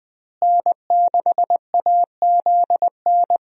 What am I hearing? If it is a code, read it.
Morse code, heard as a keyed tone, about 21 wpm, 700 Hz. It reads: N6AZN
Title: USA Call Signs - 5 Characters - Repeated 2x - 20wpm